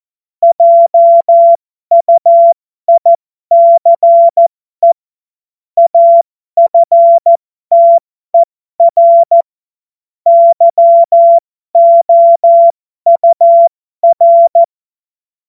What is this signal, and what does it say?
Morse code, heard as a keyed tone, about 14 wpm, spaced as for 12 wpm, 680 Hz. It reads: JUICE AFTER YOUR
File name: Morsekurs / Witze